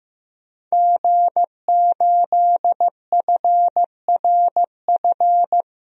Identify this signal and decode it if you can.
Morse code, heard as a keyed tone, about 15 wpm, 700 Hz. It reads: G8FRF
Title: DXCC-100 Call Signs - 5 Characters - 15wpm